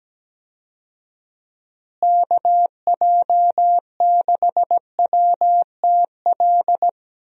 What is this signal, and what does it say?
Morse code, heard as a keyed tone, about 17 wpm, 700 Hz. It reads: KJ6WTL
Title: USA Call Signs - 6 Characters - Repeated 3x - 17wpm